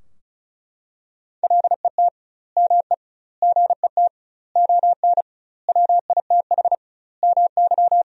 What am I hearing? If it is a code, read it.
Morse code, heard as a keyed tone, about 36 wpm, 700 Hz. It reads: LET ME GET ON WITH MY
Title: 2x Word-Spacing - Sentences from Top 300 Words - 35wpm